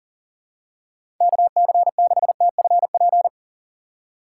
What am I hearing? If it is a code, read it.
Morse code, heard as a keyed tone, about 40 wpm, 700 Hz. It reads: KC6TFP
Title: USA Call Signs - 6 Characters - 3x Character Spacing - 40wpm